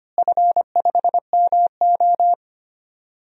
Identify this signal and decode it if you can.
Morse code, heard as a keyed tone, about 25 wpm, 700 Hz. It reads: F5MO
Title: DXCC-100 Call Signs - 4 Characters - Repeated 2x - 25wpm